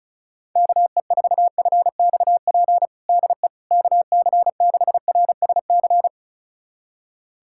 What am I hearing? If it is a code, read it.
Morse code, heard as a keyed tone, about 36 wpm, 700 Hz. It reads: KE4FXP DE KC6RSC
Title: USA Callsign DE Callsign - 6 Characters - 35wpm